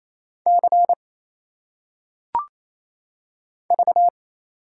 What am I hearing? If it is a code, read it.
Morse code, heard as a keyed tone, about 28 wpm, 700 Hz. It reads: C V
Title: Single Letters - Phonetic Pronunciation 28wpm